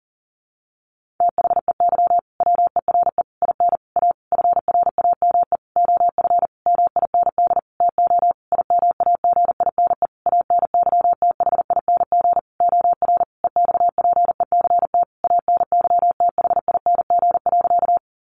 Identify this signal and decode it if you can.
Morse code, heard as a keyed tone, about 40 wpm, 700 Hz. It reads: THEY WERE IN A FRAME OF MIND TO IMAGINE ANYTHING OR EXPECT ANYTHING.
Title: Sentences from Top 1400 Words 40wpm